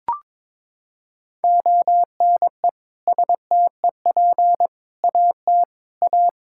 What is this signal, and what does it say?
Morse code, heard as a keyed tone, about 22 wpm, 700 Hz. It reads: ONE STEP AT A
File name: practice / 22wpm / Sentences with 3-Word Phrases - 22wpm